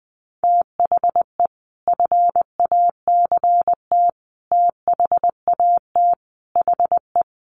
Morse code, heard as a keyed tone, about 20 wpm, 700 Hz. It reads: THE FACT THAT HE
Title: Sets of 5 Words 20wpm